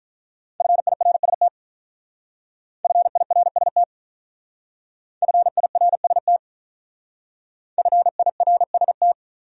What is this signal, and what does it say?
Morse code, heard as a keyed tone, about 42 wpm, 700 Hz. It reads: FIRST FIRST FIRST FIRST
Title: Speed-Racing Top 100 Words 35wpm